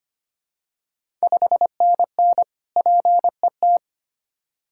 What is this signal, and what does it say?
Morse code, heard as a keyed tone, about 25 wpm, 700 Hz. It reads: 5NN PET
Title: Ontario QSO Party Exchange 25wpm